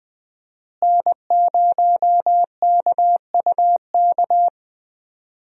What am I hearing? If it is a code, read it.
Morse code, heard as a keyed tone, about 20 wpm, 700 Hz. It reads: N0KUK
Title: USA Call Signs - 5 Characters - Repeated 2x - 20wpm